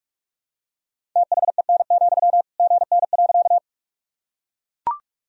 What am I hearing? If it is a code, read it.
Morse code, heard as a keyed tone, about 44 wpm, 700 Hz. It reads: THEN, GN.
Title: Sentences from Top 300 Words and Abbreviations 45wpm